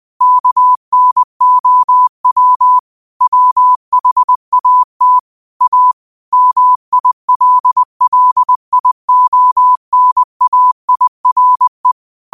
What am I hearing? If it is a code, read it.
Morse code, heard as a keyed tone, about 20 wpm, 1000 Hz. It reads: KNOW WHAT A MILLIONAIRE
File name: qotdfiles / 20wpm